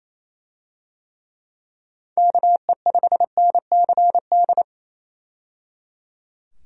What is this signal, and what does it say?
Morse code, heard as a keyed tone, about 28 wpm, 700 Hz. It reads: KE5NCD
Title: USA Call Signs - 6 Characters - 28wpm